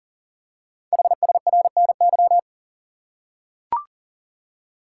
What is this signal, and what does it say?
Morse code, heard as a keyed tone, about 38 wpm, 690 Hz. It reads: HSRNY
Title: Code Groups - 5 Letters - 40wpm